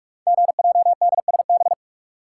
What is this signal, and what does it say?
Morse code, heard as a keyed tone, about 44 wpm, 690 Hz. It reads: GJDSB
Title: Code Groups - 5 Letters - 45wpm